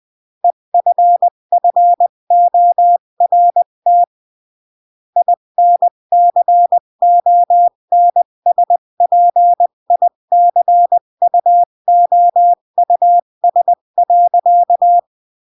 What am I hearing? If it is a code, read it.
Morse code, heard as a keyed tone, about 20 wpm, 700 Hz. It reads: EFFORT INCONSPICUOUS.
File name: CW / jodorowsky / 82 commandments 20-15